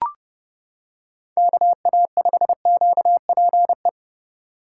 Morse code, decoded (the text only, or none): KA5QPE